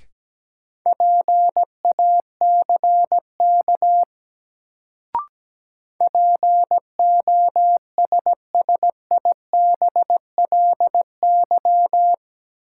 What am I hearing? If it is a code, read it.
Morse code, heard as a keyed tone, about 17 wpm, 700 Hz. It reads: PACK POSSIBLY
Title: Sentences from Top 1300 Words - New Word Drill - 17wpm